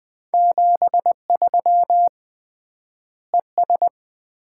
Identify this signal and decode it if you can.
Morse code, heard as a keyed tone, about 21 wpm, 700 Hz. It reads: 73 ES
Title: Common QSO Phrases - 3x Word-Spacing - 20wpm